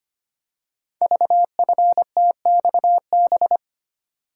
Morse code, heard as a keyed tone, about 25 wpm, 700 Hz. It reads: VFTXB
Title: Code Groups - 5 Letters - 25wpm